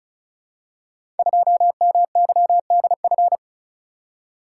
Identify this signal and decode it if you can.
Morse code, heard as a keyed tone, about 36 wpm, 700 Hz. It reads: 2MYDF
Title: Code Groups - 5 Letter:Numbers - 35wpm